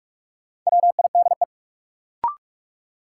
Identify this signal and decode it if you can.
Morse code, heard as a keyed tone, about 46 wpm, 700 Hz. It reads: WIDE
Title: Sentences from Top 900 Words - New Word Drill - 45wpm